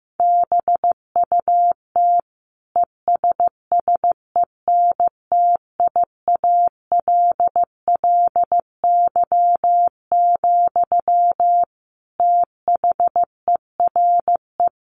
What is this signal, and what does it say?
Morse code, heard as a keyed tone, about 15 wpm, 700 Hz. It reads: BUT ESSENTIALLY, THERE
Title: Sentences from Top 1300 Words 15wpm